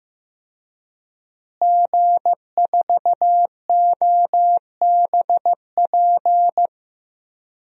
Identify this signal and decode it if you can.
Morse code, heard as a keyed tone, about 15 wpm, 700 Hz. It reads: G4OBP
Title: DXCC-100 Call Signs - 5 Characters - Repeated 2x - 15wpm